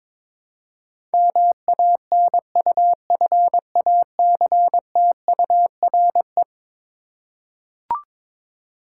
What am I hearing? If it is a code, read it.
Morse code, heard as a keyed tone, about 21 wpm, 700 Hz. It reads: MANUFACTURE
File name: practice / 22wpm / Eleven Letter Words - 22wpm